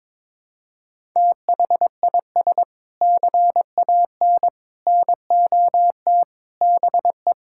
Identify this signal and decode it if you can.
Morse code, heard as a keyed tone, about 21 wpm, 700 Hz. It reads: THIS CAN NOT BE
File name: practice / 22wpm / Sentences with 3-Word Phrases - 22wpm